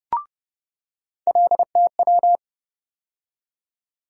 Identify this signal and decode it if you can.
Morse code, heard as a keyed tone, about 32 wpm, 700 Hz. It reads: LTW
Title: USA Call Sign Postfixes - 3 Letters 30wpm